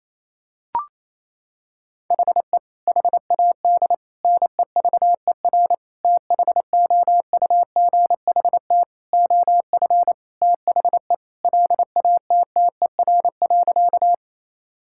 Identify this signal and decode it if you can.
Morse code, heard as a keyed tone, about 27 wpm, 700 Hz. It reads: HE HAD NEVER THOUGHT OF THE LATTER.
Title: Sentences from Top 1600 Words 28wpm